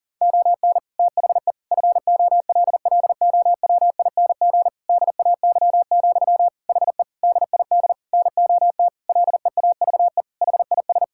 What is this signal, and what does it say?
Morse code, heard as a keyed tone, about 40 wpm, 700 Hz. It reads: ON THE FOLLOWING DAY, HE DID NOT LEAVE HIS
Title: Sentences from Top 1000 Words - Review Part 6 - 40wpm